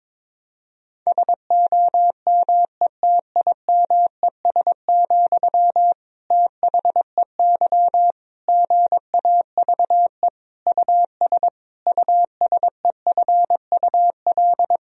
Morse code, heard as a keyed tone, about 22 wpm, 700 Hz. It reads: SOMETIMES, THEY GAVE US USEFUL
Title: Sentences from Top 1000 Words - Review Part 4 - 22wpm